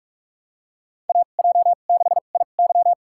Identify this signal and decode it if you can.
Morse code, heard as a keyed tone, about 46 wpm, spaced as for 33 wpm, 700 Hz. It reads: AJ6IY
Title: USA Call Signs - 5 Characters - 2x Character Spacing - 45wpm